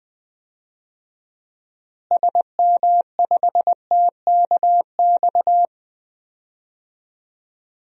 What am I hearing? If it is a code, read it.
Morse code, heard as a keyed tone, about 21 wpm, 700 Hz. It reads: SM5TKX
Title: DXCC-100 Call Signs - 6 Characters - Repeated 2x - 20wpm